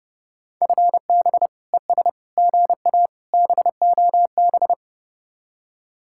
Morse code, heard as a keyed tone, about 29 wpm, 700 Hz. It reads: FB ES GA BOB